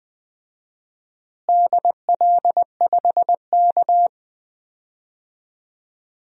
Morse code, heard as a keyed tone, about 20 wpm, 700 Hz. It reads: DL5K
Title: DXCC-100 Call Signs - 4 Characters - Repeated 2x - 20wpm